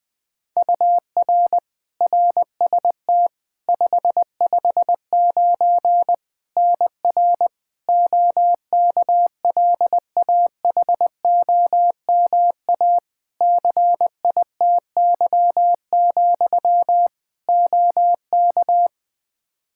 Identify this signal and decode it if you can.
Morse code, heard as a keyed tone, about 20 wpm, 700 Hz. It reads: UR RST 559 NR OKLAHOMA CITY, OK
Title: Practice QSO - Exchange 1 - Beginning - 20wpm